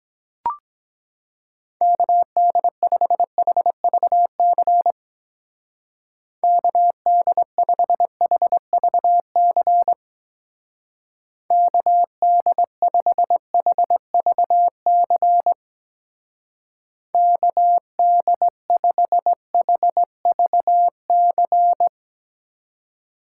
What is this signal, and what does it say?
Morse code, heard as a keyed tone, about 22 wpm, 700 Hz. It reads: KD5HVC KD5HVC KD5HVC KD5HVC